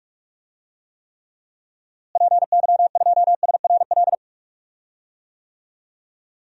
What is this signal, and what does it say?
Morse code, heard as a keyed tone, about 45 wpm, 700 Hz. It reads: PY2SRL
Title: DXCC-100 Call Signs - 6 Characters - Repeated 2x - 45wpm